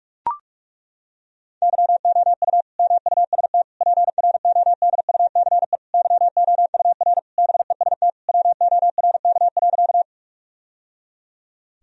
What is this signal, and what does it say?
Morse code, heard as a keyed tone, about 42 wpm, 700 Hz. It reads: YOU MUST PRODUCE YOUR BEST WORK.
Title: Sentences from Top 1000 Words - Review Part 3 - 45wpm